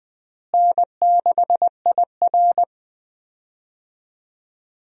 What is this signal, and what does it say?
Morse code, heard as a keyed tone, about 20 wpm, 700 Hz. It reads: N6IR
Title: USA Call Signs - 4 Characters - Repeated 2x - 20wpm